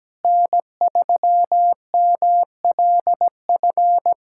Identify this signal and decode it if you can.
Morse code, heard as a keyed tone, about 17 wpm, 700 Hz. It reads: N3MLF
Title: SOTA QSO Practice 17wpm